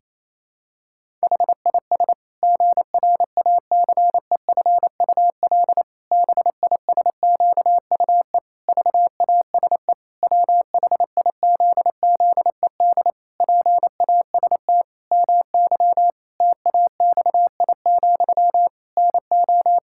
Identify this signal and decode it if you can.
Morse code, heard as a keyed tone, about 28 wpm, 700 Hz. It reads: HIS GRACEFUL BISQUE VASE WHIZZED PAST MY TAXI, NO
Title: Pangram Challenge v2 - 28wpm